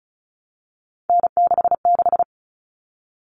Morse code, heard as a keyed tone, about 34 wpm, 700 Hz. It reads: N66
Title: All Cut Numbers - 3-Digit 35wpm